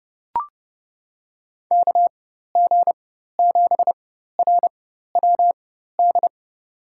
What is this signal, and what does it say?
Morse code, heard as a keyed tone, about 30 wpm, spaced as for 14 wpm, 700 Hz. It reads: KG7RWD
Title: USA Call Signs - 6 Characters - 4x Character Spacing - 30wpm